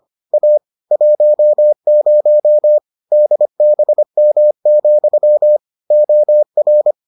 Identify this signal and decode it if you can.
Morse code, heard as a keyed tone, about 25 wpm, 600 Hz. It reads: A 10 DBM, OR